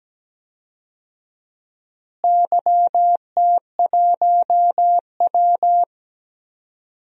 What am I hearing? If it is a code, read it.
Morse code, heard as a keyed tone, about 17 wpm, 700 Hz. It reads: YT1W
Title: DXCC-100 Call Signs - 4 Characters - Repeated 2x - 17wpm